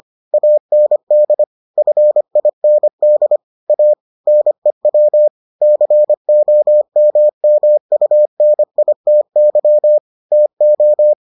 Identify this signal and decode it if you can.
Morse code, heard as a keyed tone, about 25 wpm, 600 Hz. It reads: AND FIND A NEW COMMUNITY TO